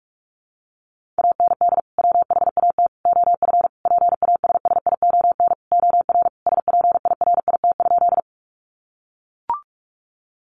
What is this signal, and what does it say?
Morse code, heard as a keyed tone, about 44 wpm, 700 Hz. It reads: AND WHAT OF PASSION OR SPIRIT?
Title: Sentences from Top 1400 Words 45wpm